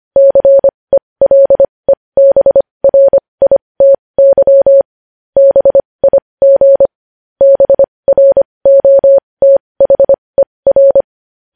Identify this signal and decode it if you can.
Morse code, heard as a keyed tone, about 25 wpm, spaced as for 20 wpm, 550 Hz. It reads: CELEBRITY BIG BROTHER
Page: news headlines in morse code at 20 wpm